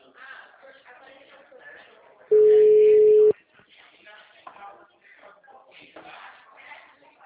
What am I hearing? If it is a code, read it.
Ringback tone (Europe, 425 Hz), with speech.
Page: Atibaia - SP 1